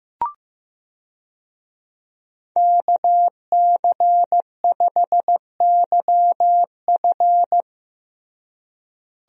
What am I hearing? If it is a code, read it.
Morse code, heard as a keyed tone, about 16 wpm, 700 Hz. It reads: KC5YF
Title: USA Call Signs - 5 Characters - Repeated 2x - 15wpm